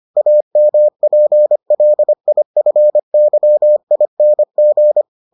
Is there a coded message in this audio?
Yes: Morse code from 0.2 to 5.0 s, heard as a keyed tone, about 25 wpm, 600 Hz. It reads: AMPLIFYING